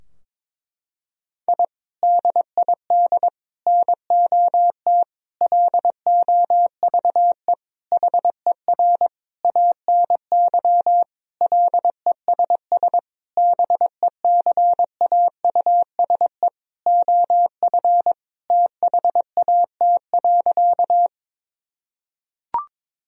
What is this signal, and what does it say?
Morse code, heard as a keyed tone, about 22 wpm, 700 Hz. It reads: I DID NOT LOVE HER ANY LESS BECAUSE OF THAT.
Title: Sentences from Top 1000 Words - Review Part 2 - 22wpm